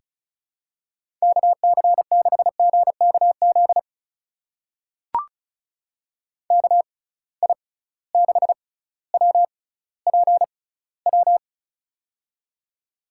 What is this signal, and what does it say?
Morse code, heard as a keyed tone, about 36 wpm, spaced as for 18 wpm, 700 Hz. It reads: KC6GKZ KI6WPW
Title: USA Call Signs - 6 Characters - 6x Character Spacing - 35wpm